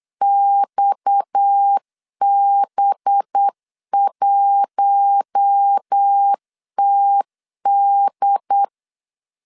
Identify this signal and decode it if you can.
Morse code, heard as a keyed tone, about 8 wpm, 800 Hz. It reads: XB1TD
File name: CW / DET / 1 6 1500C S B 8wpm 5142